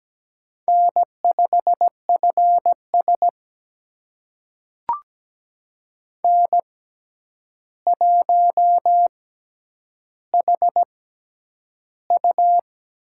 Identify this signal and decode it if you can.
Morse code, heard as a keyed tone, about 17 wpm, spaced as for 9 wpm, 700 Hz. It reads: N5FS N1HU
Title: USA Call Signs - 4 Characters - 6x Character Spacing - 17wpm